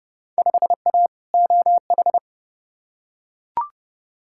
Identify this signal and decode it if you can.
Morse code, heard as a keyed tone, about 30 wpm, 700 Hz. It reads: 5A OH
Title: Field Day Exchanges Encore - 30wpm